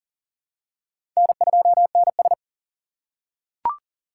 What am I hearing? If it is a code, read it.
Morse code, heard as a keyed tone, about 40 wpm, 700 Hz. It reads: N2NS
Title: K1USN SST QSO Practice - 40wpm